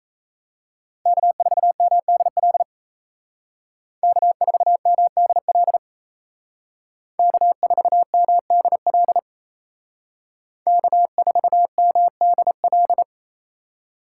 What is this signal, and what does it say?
Morse code, heard as a keyed tone, about 36 wpm, 700 Hz. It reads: K4MDL K4MDL K4MDL K4MDL